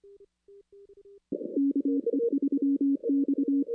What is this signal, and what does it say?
Morse code, heard as a keyed tone, about 23 wpm, 280 Hz. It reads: KE3X
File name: 2016-ARRL-DX-CW-K5TR